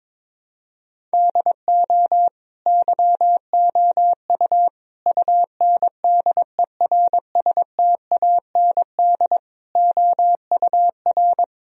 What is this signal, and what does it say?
Morse code, heard as a keyed tone, about 22 wpm, 700 Hz. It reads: DO YOU UNDERSTAND OUR
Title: Sentences from Top 1000 Words - Review Part 3 - 22wpm